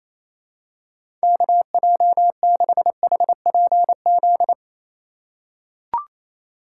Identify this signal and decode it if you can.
Morse code, heard as a keyed tone, about 28 wpm, 700 Hz. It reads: KJ6HPZ
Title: USA Call Signs - 6 Characters - 4x Character Spacing - 28wpm